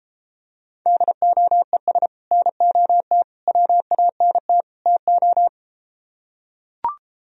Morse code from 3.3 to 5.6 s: WANT TO

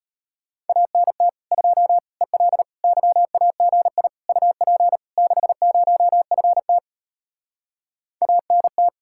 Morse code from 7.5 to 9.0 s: ANT

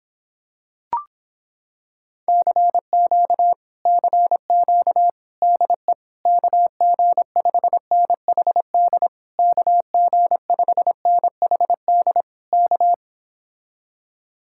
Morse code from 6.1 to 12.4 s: KG5NHD KG5NHD

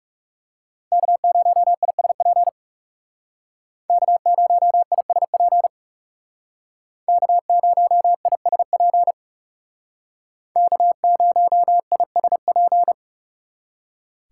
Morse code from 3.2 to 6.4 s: K0ISP

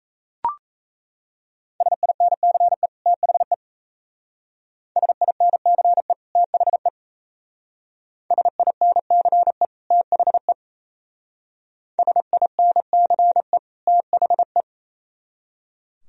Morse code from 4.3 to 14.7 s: SINCE THE SINCE THE SINCE THE

